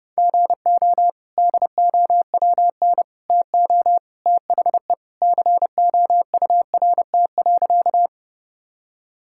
Morse code, heard as a keyed tone, about 30 wpm, 700 Hz. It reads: GO DOWN TO THE COURT.
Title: Sentences from Top 1600 Words 30wpm